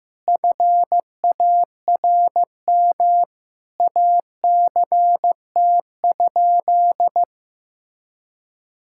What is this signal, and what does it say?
Morse code, heard as a keyed tone, about 15 wpm, 700 Hz. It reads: FARM ACT?
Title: Sentences from Top 1000 Words - Review Part 5 - 15wpm